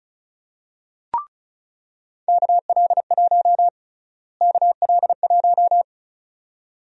Morse code, heard as a keyed tone, about 34 wpm, 700 Hz. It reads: KL1 KL1
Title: USA Call Sign Prefixes - Repeated 2x - 35wpm